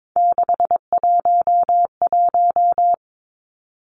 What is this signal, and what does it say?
Morse code, heard as a keyed tone, about 22 wpm, 700 Hz. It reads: 611
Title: Three Challenging Letter-Number Combinations 22wpm